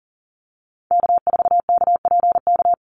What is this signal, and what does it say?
Morse code, heard as a keyed tone, about 42 wpm, 700 Hz. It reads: K4KPK
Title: NA SOTA Activator Callsigns 40wpm